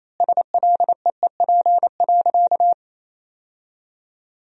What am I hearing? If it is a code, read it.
Morse code, heard as a keyed tone, about 28 wpm, 700 Hz. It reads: SLEEP.
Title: 2x Word-Spacing - Sentences from Top 500 Words - 28wpm